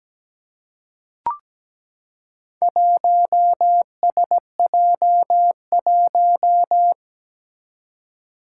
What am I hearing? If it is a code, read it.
Morse code, heard as a keyed tone, about 17 wpm, 700 Hz. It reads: 1SJ1
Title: Four Challenging Letter-Number Combinations 17wpm